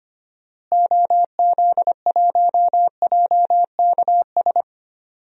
Morse code, heard as a keyed tone, about 25 wpm, 700 Hz. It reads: OZ1JKS
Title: DXCC-100 Call Signs - 6 Characters - Repeated 2x - 25wpm